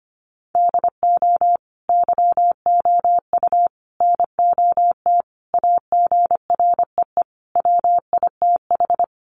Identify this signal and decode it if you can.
Morse code, heard as a keyed tone, about 25 wpm, 690 Hz. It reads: DO YOU NOT AGREE WITH
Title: Sentences from Top 1100 Words 25wpm